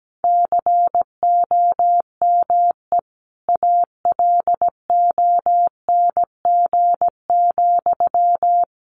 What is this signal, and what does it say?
Morse code, heard as a keyed tone, about 17 wpm, 700 Hz. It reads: COME ALONG,
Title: Sentences from Top 1500 Words 17wpm